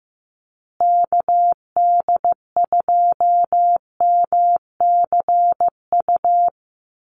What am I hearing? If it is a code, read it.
Morse code, heard as a keyed tone, about 15 wpm, 700 Hz. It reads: KD2MCU
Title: USA Call Signs Repeated 2x 15wpm